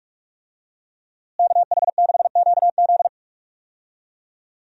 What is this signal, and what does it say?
Morse code, heard as a keyed tone, about 46 wpm, 700 Hz. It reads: KH6QZ
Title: USA Call Signs - 5 Characters - Repeated 2x - 45wpm